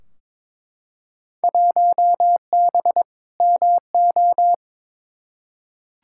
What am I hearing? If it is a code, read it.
Morse code, heard as a keyed tone, about 21 wpm, 700 Hz. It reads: 1B MO
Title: Field Day Exchanges - 22wpm